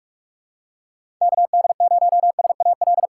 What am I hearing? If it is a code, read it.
Morse code, heard as a keyed tone, about 44 wpm, 700 Hz. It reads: KD0SAL
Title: USA Call Signs - 6 Characters - 6x Character Spacing - 45wpm